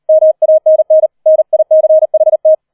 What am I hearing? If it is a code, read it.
Morse code, heard as a keyed tone, about 38 wpm, 630 Hz. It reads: MANN NICHT
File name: 004 Vierter Streich 40wpm